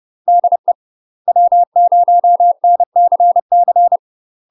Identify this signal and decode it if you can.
Morse code, heard as a keyed tone, about 30 wpm, 700 Hz. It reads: DE W0NCC
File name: UK MASTER QSOs TRAINING / 30WPM / UK MASTER QSOs 0042